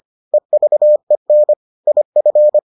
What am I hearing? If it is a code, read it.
Morse code, heard as a keyed tone, about 25 wpm, 600 Hz. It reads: EVEN IF